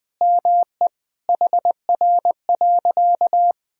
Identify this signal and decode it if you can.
Morse code, heard as a keyed tone, about 20 wpm, 700 Hz. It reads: ME HR.
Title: Sentences from Top 200 Words and Abbreviations 20wpm